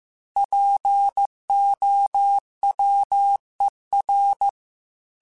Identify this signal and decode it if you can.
Morse code, heard as a keyed tone, about 15 wpm, 800 Hz. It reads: POWER
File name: Technical Subjects / Technical Subjects-mp3s / Sw Pwr Sup @13